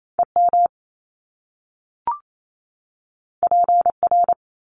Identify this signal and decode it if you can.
Morse code, heard as a keyed tone, about 28 wpm, 700 Hz. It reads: EM PR
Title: Top Two Letter Combinations in Words 28wpm